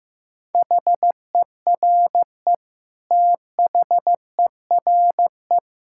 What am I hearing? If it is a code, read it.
Morse code, heard as a keyed tone, about 15 wpm, 700 Hz. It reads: HERE THERE
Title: Sentences from Top 1600 Words 15wpm